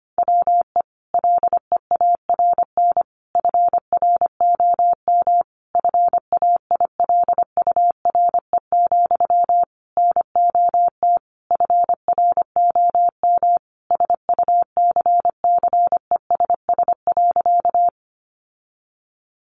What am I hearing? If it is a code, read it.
Morse code, heard as a keyed tone, about 25 wpm, 700 Hz. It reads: WE LEARN FROM FAILURE, NOT FROM SUCCESS.
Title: Sentences from Top 900 Words 25wpm